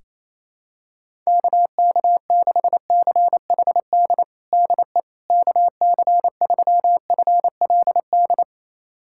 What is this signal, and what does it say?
Morse code, heard as a keyed tone, about 27 wpm, 700 Hz. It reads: KK6CHD DE KC3FLD
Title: USA Callsign DE Callsign - 6 Characters - 28wpm